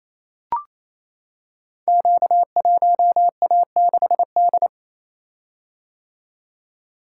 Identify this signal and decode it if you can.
Morse code, heard as a keyed tone, about 29 wpm, 700 Hz. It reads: Q1A6D
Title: Code Groups - 5 Letter:Numbers - 28wpm